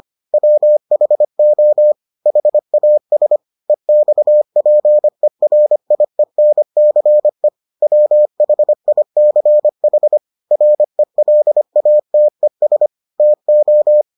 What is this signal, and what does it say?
Morse code, heard as a keyed tone, about 25 wpm, 600 Hz. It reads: WHO HAS EXPERIENCE WHICH RELATES TO